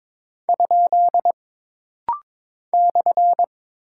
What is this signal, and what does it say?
Morse code, heard as a keyed tone, about 22 wpm, 700 Hz. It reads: ? /
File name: practice / 22wpm / Single Punctuation-Prosign - Rapid-Fire - 22wpm